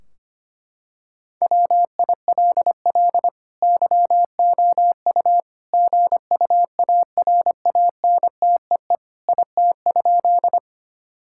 Morse code, heard as a keyed tone, about 25 wpm, 700 Hz. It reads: WILL YOU GUARANTEE IT?
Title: Sentences from Top 1600 Words 25wpm